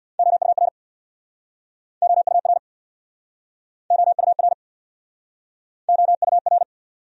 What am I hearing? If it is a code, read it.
Morse code, heard as a keyed tone, about 67 wpm, 700 Hz. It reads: OUR OUR OUR OUR